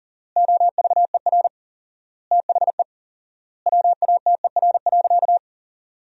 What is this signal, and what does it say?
Morse code, heard as a keyed tone, about 37 wpm, 700 Hz. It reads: OVER THE WATER.
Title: 4x Word-Spacing - Sentences from Top 300 Words - 40wpm